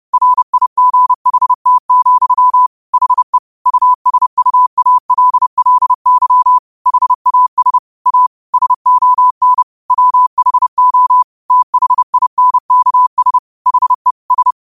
Morse code, heard as a keyed tone, about 30 wpm, 1000 Hz. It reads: RIGHT, HE USUALLY HAS A SON WHO THINKS HE S
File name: qotdfiles / 30wpm